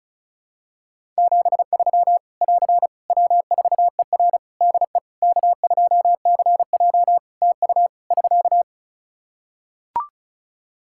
Morse code, heard as a keyed tone, about 36 wpm, 700 Hz. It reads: 73 <AR> W4ER DE K2CJ TU <SK>
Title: Practice QSO - Exchange 4 - Ending - 35wpm